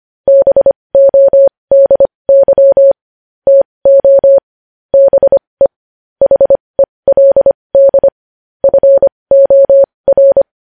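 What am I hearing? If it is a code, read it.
Morse code, heard as a keyed tone, about 25 wpm, spaced as for 20 wpm, 550 Hz. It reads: BODY TO BE HELD FOR
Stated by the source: Morse news headlines practice at 20 wpm.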